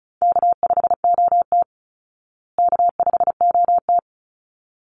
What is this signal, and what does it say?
Morse code, heard as a keyed tone, about 34 wpm, 700 Hz. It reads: K5OT K5OT
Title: Easy Call Signs Repeated 2x - 35wpm